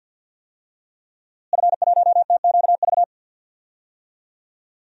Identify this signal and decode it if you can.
Morse code, heard as a keyed tone, about 47 wpm, 700 Hz. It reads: F1TQV
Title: DXCC-100 Call Signs - 5 Characters - Repeated 2x - 50wpm